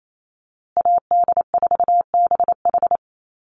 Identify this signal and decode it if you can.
Morse code, heard as a keyed tone, about 28 wpm, 700 Hz. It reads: AD4BH
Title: USA Call Signs Repeated 2x 28wpm